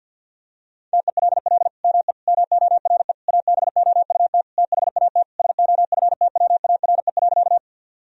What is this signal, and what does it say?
Morse code, heard as a keyed tone, about 50 wpm, 700 Hz. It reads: TELL ME MORE ABOUT THAT SOFTWARE.
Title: Sentences from Top 1000 Words - Review Part 3 - 50wpm